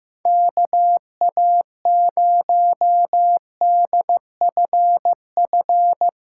Morse code, heard as a keyed tone, about 16 wpm, 700 Hz. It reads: KA0DFF
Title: USA Callsign DE Callsign 15wpm